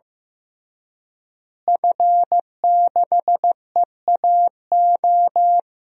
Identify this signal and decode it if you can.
Morse code, heard as a keyed tone, about 15 wpm, 700 Hz. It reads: F6EAO
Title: DXCC-100 Call Signs - 5 Characters - Repeated 2x - 15wpm